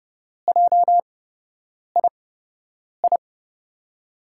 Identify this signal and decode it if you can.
Morse code, heard as a keyed tone, about 30 wpm, spaced as for 8 wpm, 700 Hz. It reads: JII
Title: USA Call Signs - 6 Characters - 8x Character Spacing - 30wpm